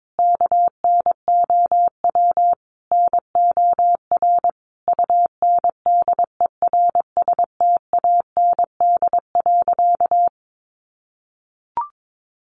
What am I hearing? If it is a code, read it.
Morse code, heard as a keyed tone, about 22 wpm, 700 Hz. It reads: KNOW NOR UNDERSTAND.
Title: Sentences from Top 1500 Words 22wpm